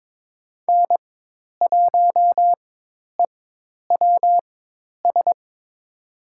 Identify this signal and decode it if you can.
Morse code, heard as a keyed tone, about 21 wpm, spaced as for 10 wpm, 700 Hz. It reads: N1EWS
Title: USA Call Signs - 5 Characters - 4x Character Spacing - 22wpm